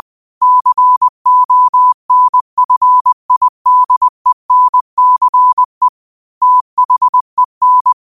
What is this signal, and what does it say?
Morse code, heard as a keyed tone, about 20 wpm, 1000 Hz. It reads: CONFIDENCE THEN
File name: qotdfiles / 20wpm